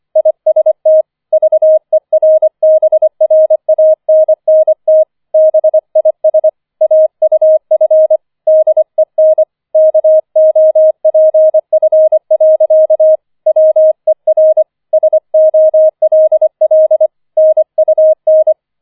Morse code, heard as a keyed tone, about 25 wpm, 630 Hz. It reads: IST VERBRANNT BIS AUF DEN KOPF. WER SOLL NUN
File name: 004 Vierter Streich 25wpm